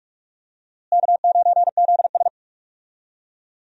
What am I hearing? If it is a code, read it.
Morse code, heard as a keyed tone, about 46 wpm, 700 Hz. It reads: K9ZS
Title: USA Call Signs - 4 Characters - Repeated 3x - 45wpm